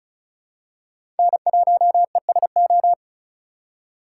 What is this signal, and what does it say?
Morse code, heard as a keyed tone, about 36 wpm, 700 Hz. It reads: N1ESO
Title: USA Call Signs - 5 Characters - 2x Character Spacing - 35wpm